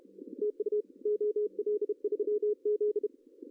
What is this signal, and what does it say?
Morse code, heard as a keyed tone, about 33 wpm, 420 Hz. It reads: TU OL3Z
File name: CW / cqww-cw-k5tr